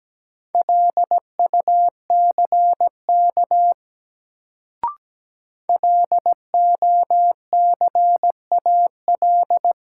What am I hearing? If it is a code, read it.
Morse code, heard as a keyed tone, about 17 wpm, 700 Hz. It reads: LUCK LOCAL
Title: Sentences from Top 1500 Words - New Word Drill Alt - 17wpm